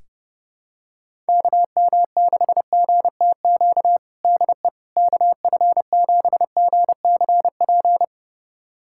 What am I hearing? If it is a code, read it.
Morse code, heard as a keyed tone, about 28 wpm, 700 Hz. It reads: KM6GTQ DE KF7GCP
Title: USA Callsign DE Callsign - 6 Characters - 30wpm